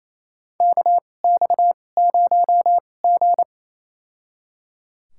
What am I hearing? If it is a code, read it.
Morse code, heard as a keyed tone, about 28 wpm, spaced as for 20 wpm, 700 Hz. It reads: KX0G
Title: USA Call Signs - 2x Character Spacing - 28wpm